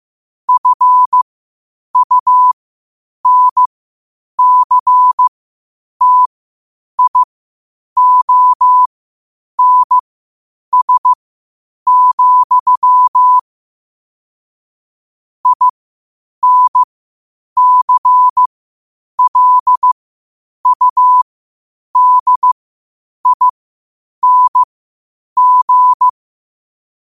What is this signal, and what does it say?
Morse code, heard as a keyed tone, about 15 wpm, spaced as for 8 wpm, 1000 Hz. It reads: FUNCTIONS, INCLUDING
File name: qotdfiles / 8wpm